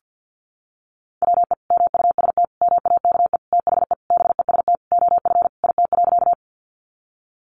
Morse code, heard as a keyed tone, about 50 wpm, 700 Hz. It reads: WE MUST MAKE THE BEST OF IT.
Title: Sentences from Top 400 Words 50wpm